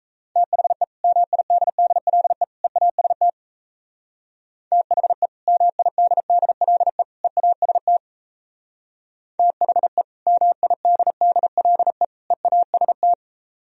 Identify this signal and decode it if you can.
Morse code, heard as a keyed tone, about 33 wpm, 700 Hz. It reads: THE MIDDLE EAST THE MIDDLE EAST THE MIDDLE EAST
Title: Speed-Racing - Sets of 3 Words 28wpm